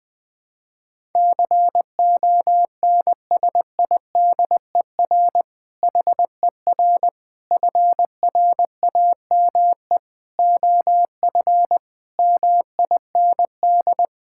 Morse code, heard as a keyed tone, about 20 wpm, 700 Hz. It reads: CONSIDER HER FRAME OF MIND
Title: Sentences from Top 1000 Words - Review Part 6 - 20wpm